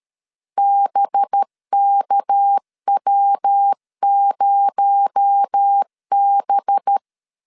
Morse code, heard as a keyed tone, about 13 wpm, 800 Hz. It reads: BKW0B
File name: LEZIONI / 6 PKX0 / 6 PKX0 12wpm 2231